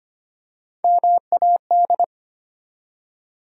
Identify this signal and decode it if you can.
Morse code, heard as a keyed tone, about 25 wpm, 700 Hz. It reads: MAD
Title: Three Letter Words - 25wpm